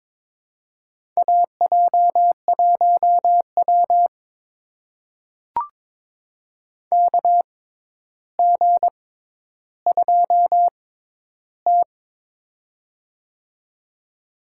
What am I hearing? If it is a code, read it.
Morse code, heard as a keyed tone, about 22 wpm, spaced as for 11 wpm, 700 Hz. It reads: AJ1W KG2T
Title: USA Call Signs - 4 Characters - 6x Character Spacing - 22wpm